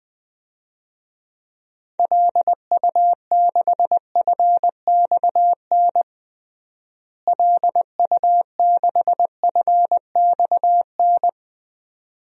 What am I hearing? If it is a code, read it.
Morse code, heard as a keyed tone, about 21 wpm, 700 Hz. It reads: LU6FXN LU6FXN
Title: DXCC-100 Call Signs - 6 Characters - Repeated 2x - 20wpm